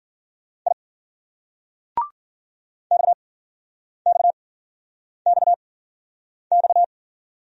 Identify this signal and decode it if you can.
Morse code, heard as a keyed tone, about 50 wpm, spaced as for 9 wpm, 700 Hz. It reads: I XXXX